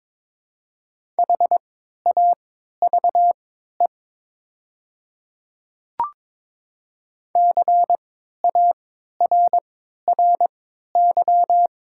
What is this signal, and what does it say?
Morse code, heard as a keyed tone, about 22 wpm, spaced as for 12 wpm, 690 Hz. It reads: HAVE CARRY
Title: 3x Character Spacing - Top 500 Words - 22wpm